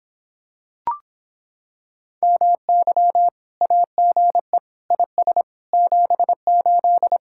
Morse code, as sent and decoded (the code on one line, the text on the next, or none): -- -.-- / .- --. / .. ... / --... ---..
MY AGE IS 78